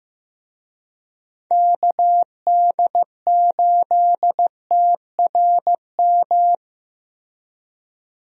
Morse code, heard as a keyed tone, about 15 wpm, 700 Hz. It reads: KD8TRM
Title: USA Call Signs - 6 Characters - Repeated 2x - 15wpm